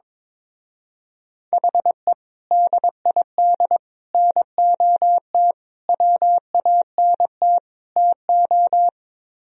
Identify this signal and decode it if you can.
Morse code, heard as a keyed tone, about 23 wpm, 700 Hz. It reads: HE DID NOT WANT TO
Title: Speed-Racing - Sets of 5 Words 22wpm